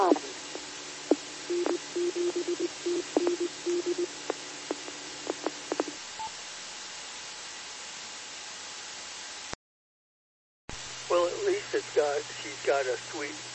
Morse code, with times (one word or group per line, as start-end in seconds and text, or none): none